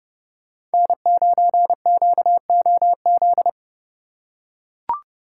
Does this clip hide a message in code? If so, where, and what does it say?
Yes: Morse code from 0.7 to 3.5 s, heard as a keyed tone, about 29 wpm, 700 Hz. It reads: N9QOZ